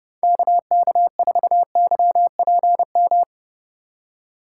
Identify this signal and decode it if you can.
Morse code, heard as a keyed tone, about 30 wpm, 700 Hz. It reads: KK4YPM
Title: USA Call Signs - 6 Characters - 3x Character Spacing - 30wpm